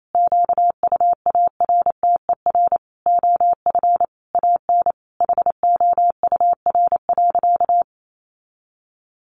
Morse code, heard as a keyed tone, about 28 wpm, 700 Hz. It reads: QUARTER OF AN HOUR.
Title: Sentences from Top 1500 Words 28wpm